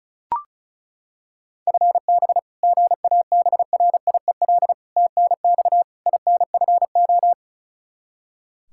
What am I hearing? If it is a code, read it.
Morse code, heard as a keyed tone, about 36 wpm, 700 Hz. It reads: FB GABRIEL TNX INFO
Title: Practice QSO - Exchange 3 - More Info - 35wpm